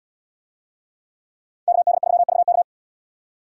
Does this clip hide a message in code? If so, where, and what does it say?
Yes: Morse code from 1.7 to 2.6 s, heard as a keyed tone, about 69 wpm, 700 Hz. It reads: KD3VK